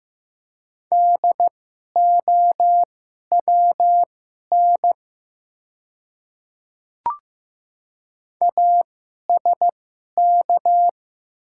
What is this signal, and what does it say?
Morse code, heard as a keyed tone, about 15 wpm, spaced as for 11 wpm, 700 Hz. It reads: DOWN ASK
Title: 2x Character Spacing - Top 200 Words - 15wpm